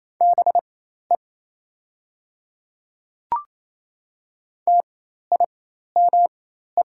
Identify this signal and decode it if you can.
Morse code, heard as a keyed tone, about 27 wpm, spaced as for 13 wpm, 690 Hz. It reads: BE TIME
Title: 4x Character Spacing - Top 100 Words - 28wpm